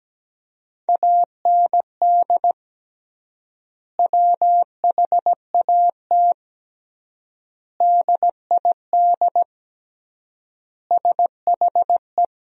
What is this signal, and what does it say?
Morse code, heard as a keyed tone, about 17 wpm, 700 Hz. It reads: AND WHAT DID SHE
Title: 3x Word-Spacing - Sentences from Top 200 Words - 17wpm